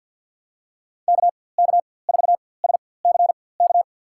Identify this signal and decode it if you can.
Morse code, heard as a keyed tone, about 52 wpm, spaced as for 23 wpm, 700 Hz. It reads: KK4SCK
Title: USA Call Signs - 6 Characters - 4x Character Spacing - 50wpm